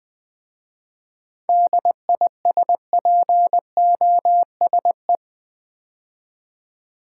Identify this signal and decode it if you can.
Morse code, heard as a keyed tone, about 20 wpm, 690 Hz. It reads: DISPOSE
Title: 7 Letter Words - First 3 Letters Given - 20wpm